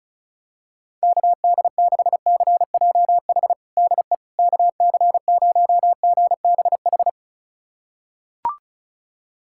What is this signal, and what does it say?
Morse code, heard as a keyed tone, about 36 wpm, 700 Hz. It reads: KD6CJH DE KC0GBH
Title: USA Callsign DE Callsign 35wpm